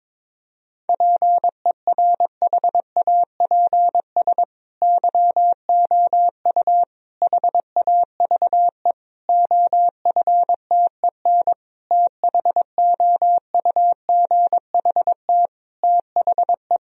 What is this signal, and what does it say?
Morse code, heard as a keyed tone, about 22 wpm, 700 Hz. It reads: PERHAPS YOU HAVE OFTEN THOUGHT THE
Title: Sentences from Top 1000 Words - Review Part 1 - 22wpm